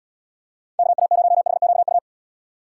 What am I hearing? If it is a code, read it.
Morse code, heard as a keyed tone, about 77 wpm, 700 Hz. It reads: KA0HQU